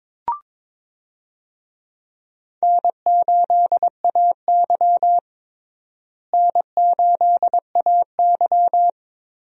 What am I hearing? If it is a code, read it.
Morse code, heard as a keyed tone, about 22 wpm, 700 Hz. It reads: N8AY N8AY
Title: USA Call Signs - 4 Characters - Repeated 3x - 22wpm